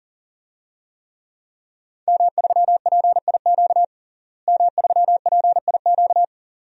Morse code, heard as a keyed tone, about 40 wpm, 700 Hz. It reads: M3PIQ M3PIQ
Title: DXCC-100 Call Signs - 5 Characters - Repeated 2x - 40wpm